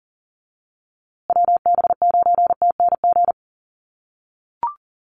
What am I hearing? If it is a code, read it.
Morse code, heard as a keyed tone, about 40 wpm, 700 Hz. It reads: WB9TNG
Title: USA Call Signs Repeated 3x 40wpm